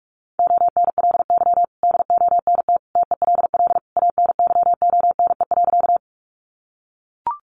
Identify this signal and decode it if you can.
Morse code, heard as a keyed tone, about 44 wpm, 700 Hz. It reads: ONLY DONT TELL ANYONE.
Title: Sentences from Top 900 Words 45wpm